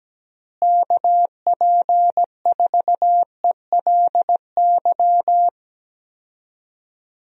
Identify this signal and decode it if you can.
Morse code, heard as a keyed tone, about 17 wpm, 700 Hz. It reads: KP4ELY
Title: USA Call Signs - 6 Characters - Repeated 3x - 17wpm